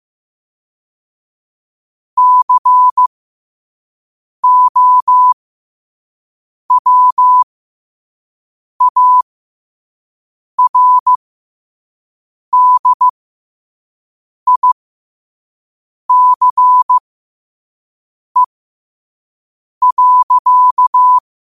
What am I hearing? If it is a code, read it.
Morse code, heard as a keyed tone, about 15 wpm, spaced as for 5 wpm, 1000 Hz. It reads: COWARDICE.
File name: qotdfiles / 5wpm